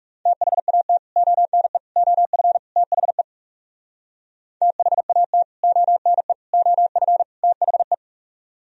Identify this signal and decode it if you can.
Morse code, heard as a keyed tone, about 37 wpm, 700 Hz. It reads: THAT ONE OF THE THAT ONE OF THE
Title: Speed-Racing - Sets of 4 Words 30wpm